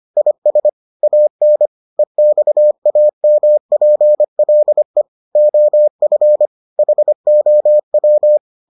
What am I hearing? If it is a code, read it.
Morse code, heard as a keyed tone, about 25 wpm, 600 Hz. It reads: IS AN EXAMPLE OF HOW